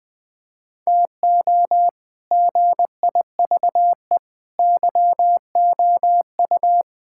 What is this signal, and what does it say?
Morse code, heard as a keyed tone, about 20 wpm, 700 Hz. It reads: TO GIVE YOU
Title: Sets of 6 Words - 20wpm